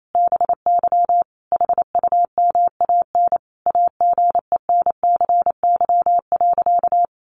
Morse code, heard as a keyed tone, about 28 wpm, 700 Hz. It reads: BY HUMAN AGENCY.
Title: Sentences from Top 1300 Words 28wpm